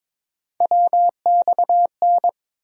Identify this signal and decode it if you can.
Morse code, heard as a keyed tone, about 22 wpm, 700 Hz. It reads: WXN
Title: USA Call Sign Postfixes - 3 Letters 22wpm